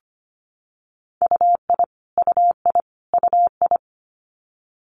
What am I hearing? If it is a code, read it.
Morse code, heard as a keyed tone, about 25 wpm, 700 Hz. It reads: UI UI UI
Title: Multi-letter Phonemes Repeated 3x 25wpm